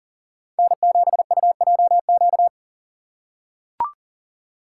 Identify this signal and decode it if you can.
Morse code, heard as a keyed tone, about 40 wpm, 700 Hz. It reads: N7UJQ
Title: USA Call Signs - 5 Characters - 2x Character Spacing - 40wpm